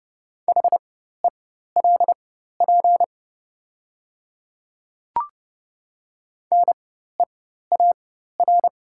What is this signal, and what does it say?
Morse code, heard as a keyed tone, about 30 wpm, spaced as for 14 wpm, 700 Hz. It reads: HELP NEAR
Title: 4x Character Spacing - Top 500 Words - 30wpm